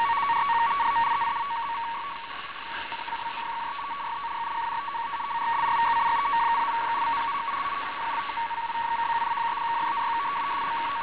Two-tone teleprinter signal, about 200 Hz shift.